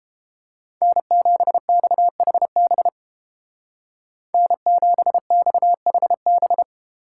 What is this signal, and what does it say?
Morse code, heard as a keyed tone, about 28 wpm, 700 Hz. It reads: N7XHB N7XHB